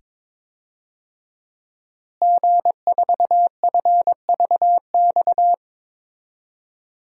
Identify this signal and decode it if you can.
Morse code, heard as a keyed tone, about 22 wpm, 700 Hz. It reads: G4FVX
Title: DXCC-100 Call Signs - 5 Characters - Repeated 2x - 22wpm